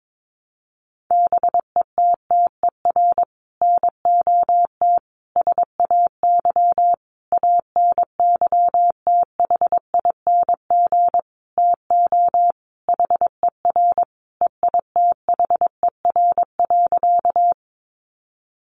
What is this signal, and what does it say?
Morse code, heard as a keyed tone, about 22 wpm, 700 Hz. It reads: BETTER NOT SAY ANYTHING TO HER EITHER.
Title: Sentences from Top 1000 Words 22wpm